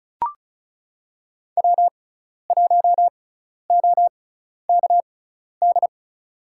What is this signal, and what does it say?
Morse code, heard as a keyed tone, about 34 wpm, spaced as for 12 wpm, 700 Hz. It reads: W1OKD
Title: USA Call Signs - 5 Characters - 6x Character Spacing - 35wpm